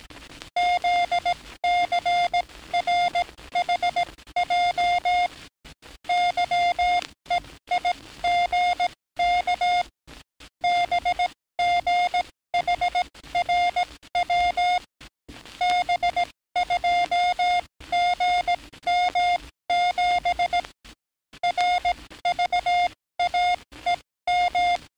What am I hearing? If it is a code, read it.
Morse code, heard as a keyed tone, about 17 wpm, spaced as for 14 wpm, 710 Hz. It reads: ZCRHJ YEIGK BGHRW B2GM7 RVAEM
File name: training / NAVPERS 11049to11060 / NAVPERS 11057 RA RB